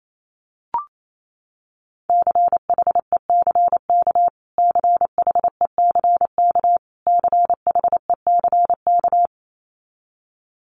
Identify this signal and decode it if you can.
Morse code, heard as a keyed tone, about 28 wpm, 700 Hz. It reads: CHECK CHECK CHECK
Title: Top 500 Words Repeated 28wpm